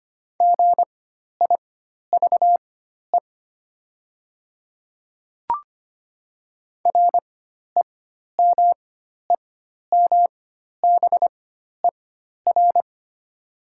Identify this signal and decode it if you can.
Morse code, heard as a keyed tone, about 25 wpm, spaced as for 12 wpm, 700 Hz. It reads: GIVE REMEMBER
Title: 4x Character Spacing - Top 500 Words - 25wpm